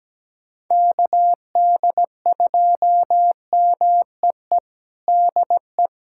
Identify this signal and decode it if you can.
Morse code, heard as a keyed tone, about 17 wpm, 700 Hz. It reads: KD2MEE DE
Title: USA Callsign DE Callsign - 6 Characters - 17wpm